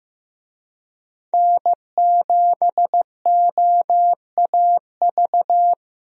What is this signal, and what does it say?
Morse code, heard as a keyed tone, about 15 wpm, 700 Hz. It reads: N7OAV
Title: USA Call Signs - 5 Characters - Repeated 2x - 15wpm